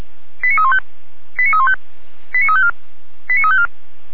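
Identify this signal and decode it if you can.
Five-tone selective call (ZVEI), heard as quick runs of five tones, tones of about 70 ms. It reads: 89315 89315 89354 89354
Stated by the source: ZVEI five-tone call received by SDR.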